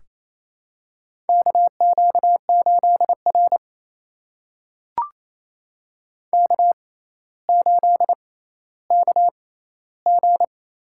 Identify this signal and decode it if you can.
Morse code, heard as a keyed tone, about 27 wpm, spaced as for 14 wpm, 700 Hz. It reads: KQ8R K8KG
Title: USA Call Signs - 4 Characters - 6x Character Spacing - 28wpm